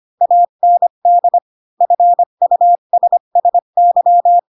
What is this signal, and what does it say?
Morse code, heard as a keyed tone, about 25 wpm, 700 Hz. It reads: AND FUSSY
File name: Pinocchio / Pinocchio ch7 20 25 ik1hge